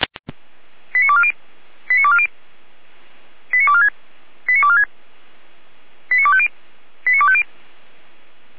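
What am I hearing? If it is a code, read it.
Five-tone selective call (ZVEI), heard as quick runs of five tones, tones of about 70 ms. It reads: 89240 89240 89246 89246 89250 89250
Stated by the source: ZVEI five-tone selective call received on an SDR.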